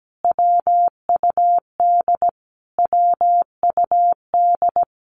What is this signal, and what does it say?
Morse code, heard as a keyed tone, about 17 wpm, 700 Hz. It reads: WUD WUD
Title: Abbreviations Repeated 3x - 17wpm